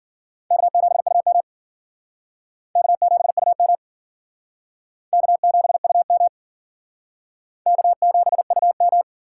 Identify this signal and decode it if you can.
Morse code, heard as a keyed tone, about 54 wpm, 700 Hz. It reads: K7UM K7UM K7UM K7UM